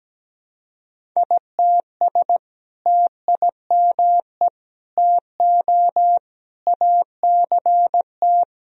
Morse code, heard as a keyed tone, about 17 wpm, 700 Hz. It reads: ITS TIME TO ACT
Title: Sentences from Top 1000 Words - Review Part 1 - 17wpm